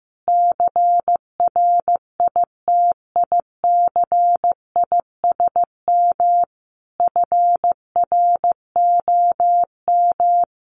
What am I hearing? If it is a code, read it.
Morse code, heard as a keyed tone, about 15 wpm, 700 Hz. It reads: CRITICISM FROM
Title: Sentences from Top 1300 Words 15wpm